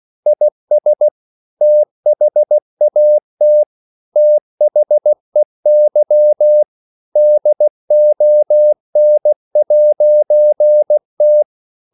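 Morse code, heard as a keyed tone, about 16 wpm, 600 Hz. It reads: IS THAT THEY DON'T
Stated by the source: English quotations sent in CW as practice at 16 words per minute.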